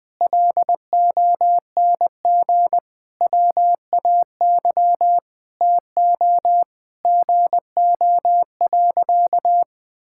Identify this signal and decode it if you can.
Morse code, heard as a keyed tone, about 21 wpm, 700 Hz. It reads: LONG WAY TO GO.
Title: Sentences with 5-Word Phrases - Spoken Beginning - 20wpm